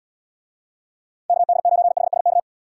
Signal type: Morse code

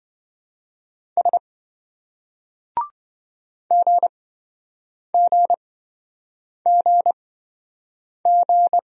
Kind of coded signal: Morse code

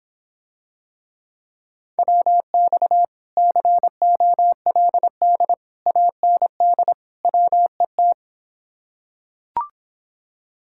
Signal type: Morse code

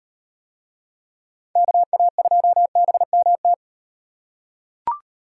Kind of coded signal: Morse code

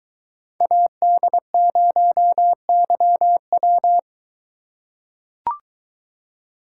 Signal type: Morse code

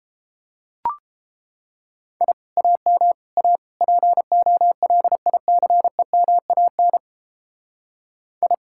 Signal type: Morse code

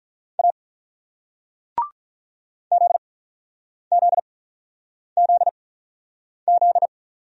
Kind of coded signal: Morse code

